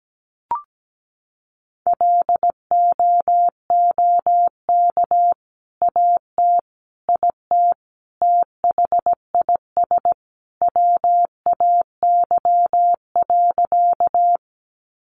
Morse code, heard as a keyed tone, about 17 wpm, 700 Hz. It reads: LOOK AT IT THIS WAY.
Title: Sentences from Top 100 Words 17wpm